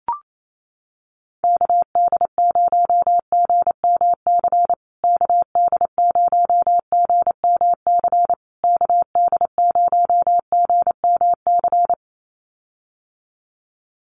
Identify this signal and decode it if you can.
Morse code, heard as a keyed tone, about 28 wpm, 700 Hz. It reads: KD0GMC KD0GMC KD0GMC
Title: USA Call Signs Repeated 3x 28wpm